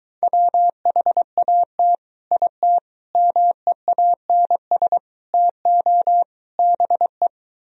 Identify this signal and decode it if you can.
Morse code, heard as a keyed tone, about 23 wpm, 700 Hz. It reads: WHAT IT MEANS TO BE